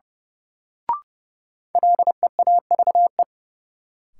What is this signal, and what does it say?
Morse code, heard as a keyed tone, about 30 wpm, 700 Hz. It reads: LEAVE
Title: Sentences from Top 400 Words - New Word Drill - 30wpm